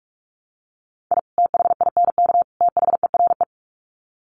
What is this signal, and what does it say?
Morse code, heard as a keyed tone, about 44 wpm, 700 Hz. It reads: I THINK THERE
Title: Sets of 3 Words 45wpm